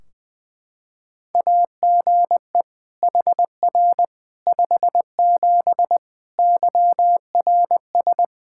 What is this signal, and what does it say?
Morse code, heard as a keyed tone, about 20 wpm, 700 Hz. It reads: AGE HR 57 YRS
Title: Practice QSO - Exchange 3 - More Info - 20wpm